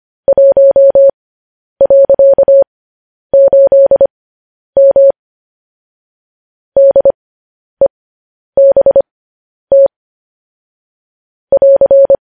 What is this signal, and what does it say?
Morse code, heard as a keyed tone, about 25 wpm, spaced as for 10 wpm, 550 Hz. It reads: 1.8M DEBT <AR>